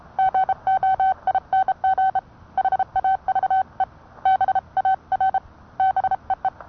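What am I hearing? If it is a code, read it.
Morse code, heard as a keyed tone, about 30 wpm, 760 Hz. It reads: GOING HAVE BAR BEE
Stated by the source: Morse code sent by N1EA on a Vibroplex key.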